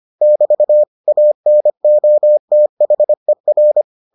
Morse code, heard as a keyed tone, about 25 wpm, 600 Hz. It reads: <BT> ANOTHER